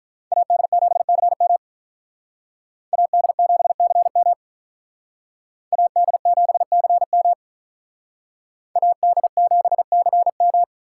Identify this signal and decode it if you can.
Morse code, heard as a keyed tone, about 45 wpm, 700 Hz. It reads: AD7CM AD7CM AD7CM AD7CM